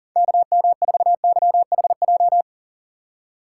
Morse code, heard as a keyed tone, about 37 wpm, 700 Hz. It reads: KM4YHJ
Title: USA Call Signs - 6 Characters - 6x Character Spacing - 40wpm